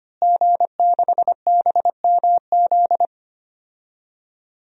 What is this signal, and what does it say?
Morse code, heard as a keyed tone, about 25 wpm, 700 Hz. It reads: G6BMZ
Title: DXCC-100 Call Signs - 5 Characters - Repeated 3x - 25wpm